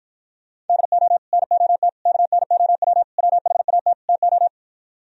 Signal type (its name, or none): Morse code